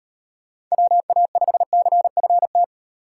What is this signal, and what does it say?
Morse code, heard as a keyed tone, about 38 wpm, 700 Hz. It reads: WA5CFT